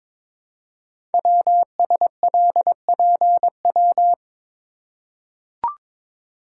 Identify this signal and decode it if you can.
Morse code, heard as a keyed tone, about 23 wpm, 700 Hz. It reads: WSLPW
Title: Code Groups - 5 Letters - 22wpm